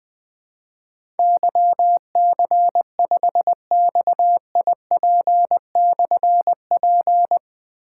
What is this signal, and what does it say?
Morse code, heard as a keyed tone, about 20 wpm, 700 Hz. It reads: YC5XIP/P
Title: Difficult Call Signs 20wpm